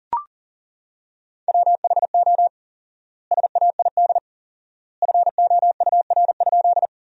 Morse code, heard as a keyed tone, about 40 wpm, 700 Hz. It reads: WHO SAID FOUR?
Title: 4x Word-Spacing - Sentences from Top 300 Words - 40wpm